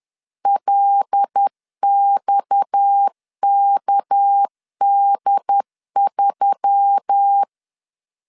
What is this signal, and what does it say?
Morse code, heard as a keyed tone, about 10 wpm, 800 Hz. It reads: LXKD3
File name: LEZIONI / 10 SIGNS / 10D COMA 10wpm 3013